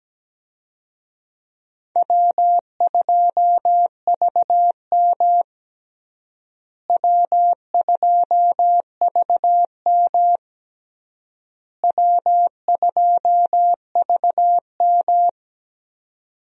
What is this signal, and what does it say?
Morse code, heard as a keyed tone, about 17 wpm, 700 Hz. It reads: W2VM W2VM W2VM
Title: USA Call Signs - 4 Characters - Repeated 3x - 17wpm